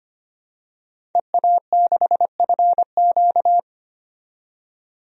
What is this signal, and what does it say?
Morse code, heard as a keyed tone, about 25 wpm, 700 Hz. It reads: EA6FQ
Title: DXCC-100 Call Signs - 5 Characters - 25wpm